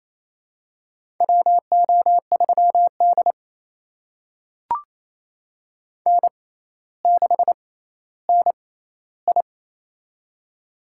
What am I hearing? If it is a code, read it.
Morse code, heard as a keyed tone, about 28 wpm, spaced as for 14 wpm, 700 Hz. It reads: WO3D N6NI
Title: USA Call Signs - 6x Character Spacing - 28wpm